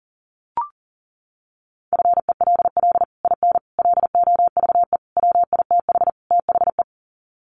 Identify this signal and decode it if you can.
Morse code, heard as a keyed tone, about 37 wpm, 700 Hz. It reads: FELL IN LOVE WITH THE
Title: Sets of 5 Words Encore2 40wpm